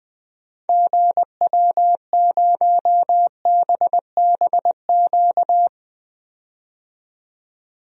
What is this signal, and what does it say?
Morse code, heard as a keyed tone, about 20 wpm, 700 Hz. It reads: GW0BBQ
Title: DXCC-100 Call Signs - 6 Characters - 20wpm